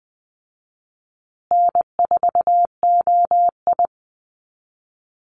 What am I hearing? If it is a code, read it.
Morse code, heard as a keyed tone, about 19 wpm, 700 Hz. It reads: N4OI
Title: Easy Call Signs Repeated 3x - 20wpm